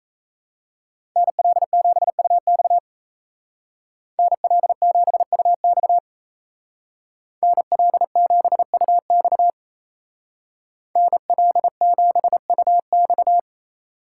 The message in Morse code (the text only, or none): NL7UX NL7UX NL7UX NL7UX